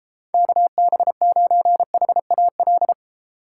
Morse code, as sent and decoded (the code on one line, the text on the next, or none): -.- -... ----. .... .- .-..
KB9HAL